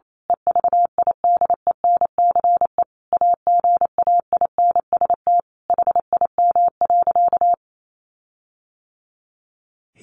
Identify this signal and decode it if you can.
Morse code, heard as a keyed tone, about 28 wpm, 700 Hz. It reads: EVIDENCE AGAINST HIM.
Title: Sentences from Top 1300 Words 28wpm